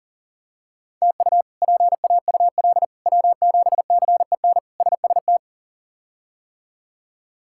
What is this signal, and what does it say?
Morse code, heard as a keyed tone, about 40 wpm, 700 Hz. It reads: TU PAUL W7CEN SST
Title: K1USN SST QSO Practice - 40wpm